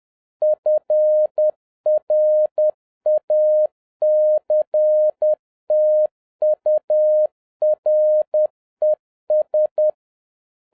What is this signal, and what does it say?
Morse code, heard as a keyed tone, about 10 wpm, 610 Hz. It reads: FRACTURES
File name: 2011-07-mp3 / n56 BACK FROM A BIG ACCIDENT 10wpm